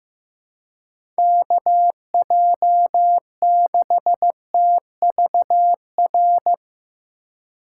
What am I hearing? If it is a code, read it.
Morse code, heard as a keyed tone, about 15 wpm, 700 Hz. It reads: KJ6TVR
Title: USA Call Signs - 6 Characters - 4x Character Spacing - 15wpm